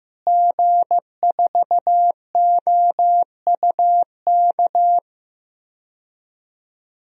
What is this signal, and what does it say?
Morse code, heard as a keyed tone, about 15 wpm, 700 Hz. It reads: G4OUK
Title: DXCC-100 Call Signs - 5 Characters - Repeated 3x - 15wpm